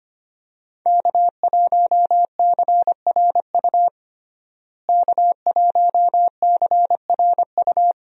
Morse code, heard as a keyed tone, about 25 wpm, 700 Hz. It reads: K1CRU K1CRU
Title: USA Call Signs - 5 Characters - Repeated 3x - 25wpm